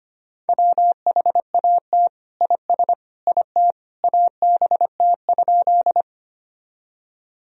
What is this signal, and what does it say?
Morse code, heard as a keyed tone, about 25 wpm, 700 Hz. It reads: WHAT IS IT ABT?
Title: Sentences from Top 100 Words and Abbreviations 25wpm